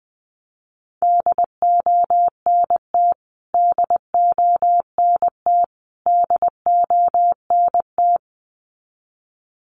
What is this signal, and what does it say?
Morse code, heard as a keyed tone, about 20 wpm, 700 Hz. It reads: DONT DONT DONT
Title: Top 300 Words Repeated 20wpm